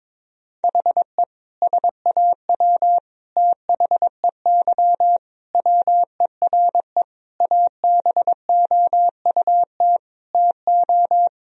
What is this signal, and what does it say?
Morse code, heard as a keyed tone, about 22 wpm, 700 Hz. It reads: HE SAW THEY WERE ABOUT TO
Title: Sentences from Top 1000 Words - Review Part 4 - 22wpm